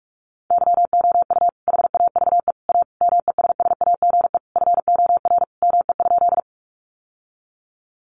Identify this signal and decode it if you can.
Morse code, heard as a keyed tone, about 44 wpm, 700 Hz. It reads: YOU HAVE A MESSAGE FOR ME?
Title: Sentences from Top 1100 Words 45wpm